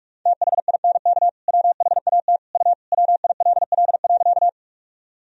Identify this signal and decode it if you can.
Morse code, heard as a keyed tone, about 43 wpm, 700 Hz. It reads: THINK WHAT U WILL.
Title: Sentences from Top 200 Words and Abbreviations 45wpm